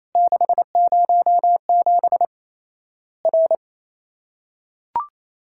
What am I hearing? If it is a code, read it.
Morse code, heard as a keyed tone, about 27 wpm, 700 Hz. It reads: 607 R
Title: Worked All Europe DX Contest - QTC Exchange - 28wpm